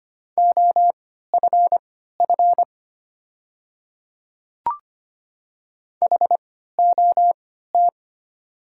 Morse code, heard as a keyed tone, about 25 wpm, spaced as for 14 wpm, 690 Hz. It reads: OFF HOT
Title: 3x Character Spacing - Top 300 Words - 25wpm